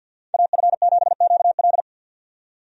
Morse code, heard as a keyed tone, about 50 wpm, 700 Hz. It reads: AF7QL
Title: USA Call Signs - 5 Characters - 50wpm